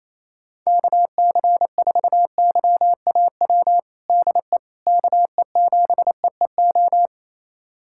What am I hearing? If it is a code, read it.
Morse code, heard as a keyed tone, about 28 wpm, 700 Hz. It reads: KC4YAW DE KE7EEO
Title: USA Callsign DE Callsign 28wpm